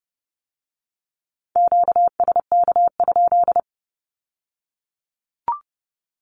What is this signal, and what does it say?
Morse code, heard as a keyed tone, about 30 wpm, 700 Hz. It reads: QSK?
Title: Q-Codes with Spoken Meaning - 30wpm